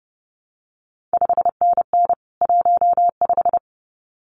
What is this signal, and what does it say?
Morse code, heard as a keyed tone, about 30 wpm, 700 Hz. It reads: 5NN 15
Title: CQ WW DX Contest Exchange - 30wpm